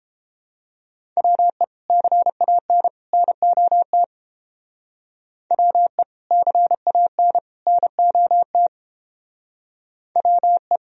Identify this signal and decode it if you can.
Morse code, heard as a keyed tone, about 28 wpm, 700 Hz. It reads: WE CAN NOT WE CAN NOT WE